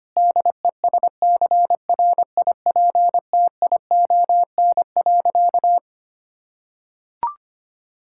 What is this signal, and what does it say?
Morse code, heard as a keyed tone, about 25 wpm, 700 Hz. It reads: DESCRIPTION.
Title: Sentences from Top 1600 Words 25wpm